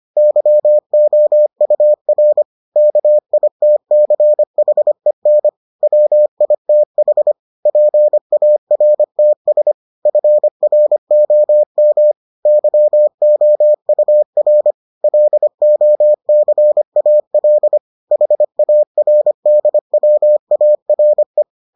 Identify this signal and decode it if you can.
Morse code, heard as a keyed tone, about 25 wpm, 600 Hz. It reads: YOUR KITCHEN WITH PARTS FROM YOUR LOCAL HARDWARE